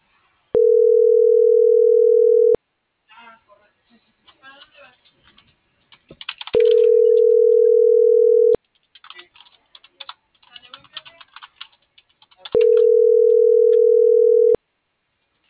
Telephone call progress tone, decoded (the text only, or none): ringback tone